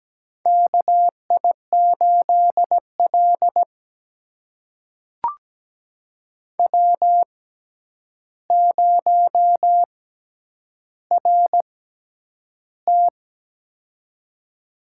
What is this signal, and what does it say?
Morse code, heard as a keyed tone, about 17 wpm, spaced as for 9 wpm, 700 Hz. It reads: KI8L W0RT
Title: USA Call Signs - 4 Characters - 6x Character Spacing - 17wpm